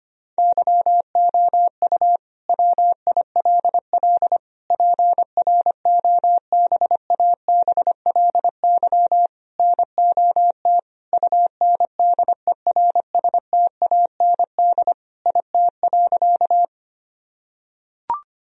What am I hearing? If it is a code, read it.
Morse code, heard as a keyed tone, about 25 wpm, 700 Hz. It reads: YOU WILL PROBABLY NOT UNDERSTAND IT.
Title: Sentences from Top 1000 Words - Review Part 4 - 25wpm